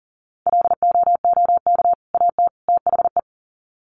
Morse code, heard as a keyed tone, about 40 wpm, 700 Hz. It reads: LOOK AT THE
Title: Sets of 3 Words 40wpm